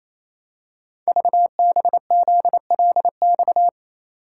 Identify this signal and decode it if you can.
Morse code, heard as a keyed tone, about 28 wpm, 700 Hz. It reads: VBZLX
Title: Code Groups - 5 Letters - 28wpm